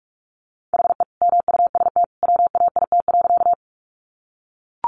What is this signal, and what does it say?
Morse code, heard as a keyed tone, about 44 wpm, 700 Hz. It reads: HE MUST WAIT.
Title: Sentences from Top 500 Words 45wpm